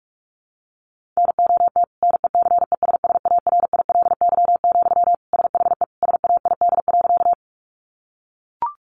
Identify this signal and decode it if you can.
Morse code, heard as a keyed tone, about 44 wpm, 700 Hz. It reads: NOT NECESSARILY, SHE SAID.
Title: Sentences from Top 1400 Words 45wpm